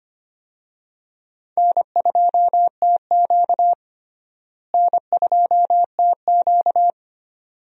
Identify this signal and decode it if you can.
Morse code, heard as a keyed tone, about 25 wpm, 700 Hz. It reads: N2TQ N2TQ
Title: USA Call Signs - 4 Characters - Repeated 2x - 25wpm